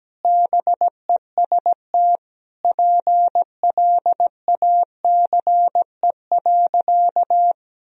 Morse code, heard as a keyed tone, about 17 wpm, 700 Hz. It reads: BEST PLACE.
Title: Sentences from Top 1000 Words - Review Part 4 - 17wpm